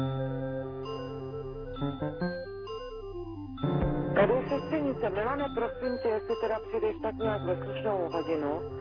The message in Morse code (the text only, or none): none